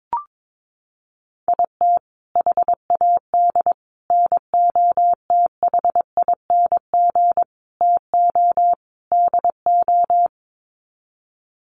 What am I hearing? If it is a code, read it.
Morse code, heard as a keyed tone, about 22 wpm, 700 Hz. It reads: IT HAD NOTHING TO DO
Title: Sets of 5 Words Encore2 22wpm